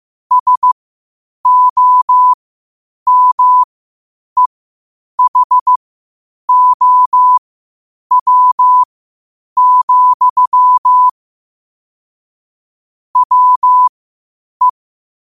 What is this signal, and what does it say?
Morse code, heard as a keyed tone, about 15 wpm, spaced as for 8 wpm, 1000 Hz. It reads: SOMEHOW, WE
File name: qotdfiles / 8wpm